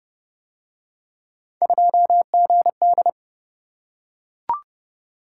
Morse code, heard as a keyed tone, about 28 wpm, 700 Hz. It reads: 2GD